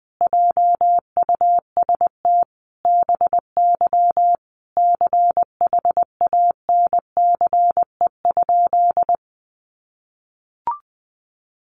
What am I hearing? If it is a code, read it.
Morse code, heard as a keyed tone, about 20 wpm, 700 Hz. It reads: JUST BY CHANCE?
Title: Sentences from Top 800 Words 20wpm